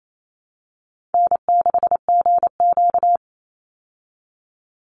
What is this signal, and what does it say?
Morse code, heard as a keyed tone, about 28 wpm, 700 Hz. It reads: N6GQ
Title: Easy Call Signs Repeated 3x - 28wpm